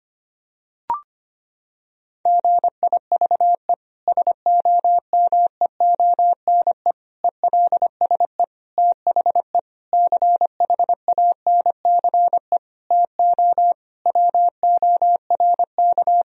Morse code, heard as a keyed tone, about 25 wpm, 700 Hz. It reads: GIVE SOMEONE ELSE THE CHANCE TO WORK
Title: Sentences from Top 1000 Words - Review Part 4 - 25wpm